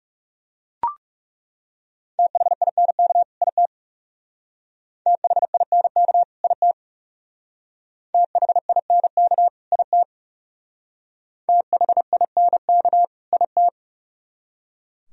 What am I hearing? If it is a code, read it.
Morse code, heard as a keyed tone, about 38 wpm, 700 Hz. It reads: THINK IT THINK IT THINK IT THINK IT